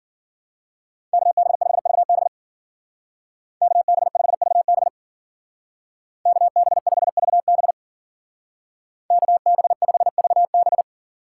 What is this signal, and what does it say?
Morse code, heard as a keyed tone, about 48 wpm, 700 Hz. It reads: KB5VB KB5VB KB5VB KB5VB
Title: Speed Racing - USA 2x2 Call Signs 40wpm